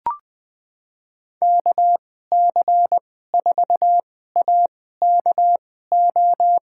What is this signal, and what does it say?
Morse code, heard as a keyed tone, about 20 wpm, spaced as for 15 wpm, 700 Hz. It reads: KC4AKO
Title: USA Call Signs - 6 Characters - 2x Character Spacing - 20wpm